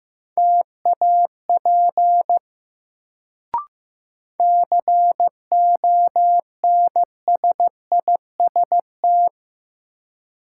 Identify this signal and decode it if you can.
Morse code, heard as a keyed tone, about 15 wpm, 700 Hz. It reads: TAP CONSIST
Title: Sentences from Top 1400 Words - New Word Drill - 15wpm